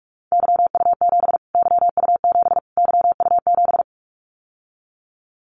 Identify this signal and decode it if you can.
Morse code, heard as a keyed tone, about 44 wpm, 700 Hz. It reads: YU7 YU7 YU7
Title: Callsign Prefixes for Top 25 DXCC - Repeated 3x - 45wpm